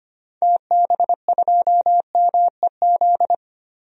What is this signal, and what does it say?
Morse code, heard as a keyed tone, about 25 wpm, 700 Hz. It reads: TB2MEZ
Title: DXCC-100 Call Signs - 6 Characters - Repeated 3x - 25wpm